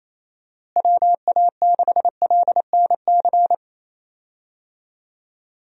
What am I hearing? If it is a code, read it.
Morse code, heard as a keyed tone, about 28 wpm, 700 Hz. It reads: WA6LNC
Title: USA Call Signs - 6 Characters - Repeated 3x - 28wpm